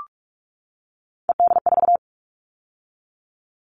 Morse code, heard as a keyed tone, about 44 wpm, 700 Hz. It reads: ED4
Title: All Cut Numbers - 3-Digit 45wpm